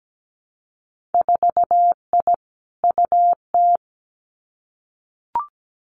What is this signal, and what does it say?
Morse code, heard as a keyed tone, about 17 wpm, 700 Hz. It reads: 4I UT
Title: Winter Field Day Exchanges 17wpm